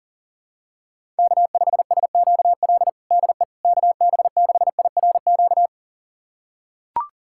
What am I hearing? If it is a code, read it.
Morse code, heard as a keyed tone, about 40 wpm, 700 Hz. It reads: K5SQL DE KB6IRQ
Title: USA Callsign DE Callsign 40wpm